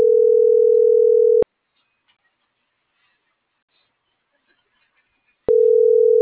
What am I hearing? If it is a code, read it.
Ringback tone (North America), 440 + 480 Hz.